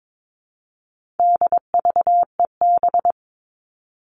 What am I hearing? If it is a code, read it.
Morse code, heard as a keyed tone, about 22 wpm, 700 Hz. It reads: DVEB
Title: All Cut Numbers - 4-Digit 22wpm